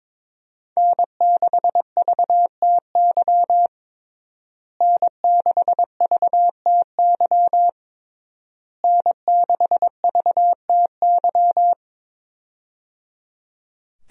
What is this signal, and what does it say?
Morse code, heard as a keyed tone, about 22 wpm, 700 Hz. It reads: N6VTY N6VTY N6VTY
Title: USA Call Signs - 5 Characters - Repeated 3x - 22wpm